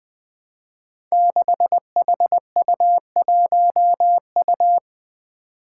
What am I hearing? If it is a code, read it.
Morse code, heard as a keyed tone, about 20 wpm, 700 Hz. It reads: 6HU1U
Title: Five Challenging Letter-Number Combinations 20wpm